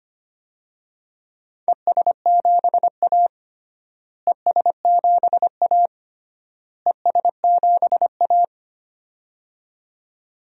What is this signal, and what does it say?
Morse code, heard as a keyed tone, about 25 wpm, 700 Hz. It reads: ES7A ES7A ES7A
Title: DXCC-100 Call Signs - 4 Characters - Repeated 3x - 25wpm